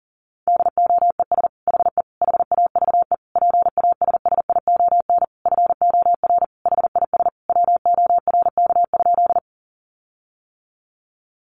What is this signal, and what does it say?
Morse code, heard as a keyed tone, about 40 wpm, 700 Hz. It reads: DOES HE HAVE PASSION FOR HIS WORK?
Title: Sentences from Top 1300 Words 40wpm